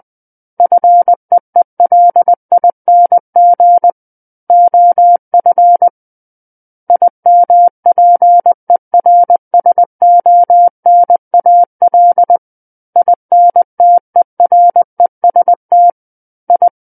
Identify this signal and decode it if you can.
Morse code, heard as a keyed tone, about 20 wpm, 700 Hz. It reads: FEELING OF IMPERSONAL INTEREST I
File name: 20 WPM